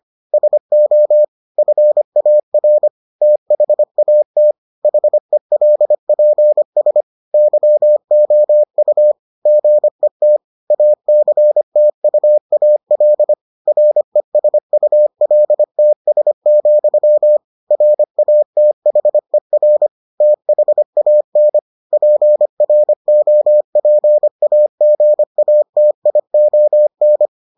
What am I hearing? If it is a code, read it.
Morse code, heard as a keyed tone, about 25 wpm, 600 Hz. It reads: SO FAR THAT HELPS YOU GET ACTUAL RESULTS, RATHER THAN PROPAGATION